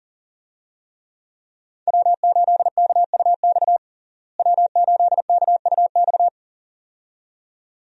Morse code, heard as a keyed tone, about 40 wpm, 700 Hz. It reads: W8KUX W8KUX
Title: USA Call Signs - 5 Characters - Repeated 2x - 40wpm